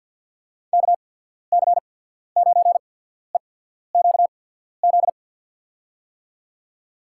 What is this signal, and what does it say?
Morse code, heard as a keyed tone, about 48 wpm, spaced as for 14 wpm, 690 Hz. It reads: KC9EQZ
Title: USA Call Signs - 6 Characters - 8x Character Spacing - 50wpm